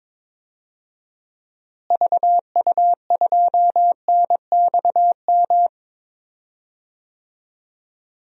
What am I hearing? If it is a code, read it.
Morse code, heard as a keyed tone, about 22 wpm, 700 Hz. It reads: VU2NXM
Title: DXCC-100 Call Signs - 6 Characters - Repeated 3x - 22wpm